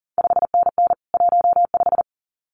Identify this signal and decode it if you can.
Morse code, heard as a keyed tone, about 40 wpm, 700 Hz. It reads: 5NN 15
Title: CQ WW DX Contest Exchange Encore - 40wpm